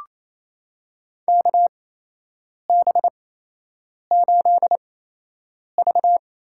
Morse code, heard as a keyed tone, about 28 wpm, spaced as for 8 wpm, 700 Hz. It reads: KB8V
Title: USA Call Signs - 6 Characters - 8x Character Spacing - 28wpm